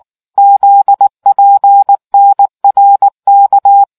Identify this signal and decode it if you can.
Morse code, heard as a keyed tone, about 19 wpm, 800 Hz. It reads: ZPNRK